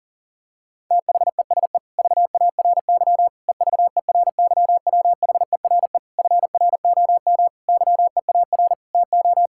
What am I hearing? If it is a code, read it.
Morse code, heard as a keyed tone, about 40 wpm, 700 Hz. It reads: THESE VARY EVERYWHERE FROM YEAR TO
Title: Sentences from Top 1600 Words 40wpm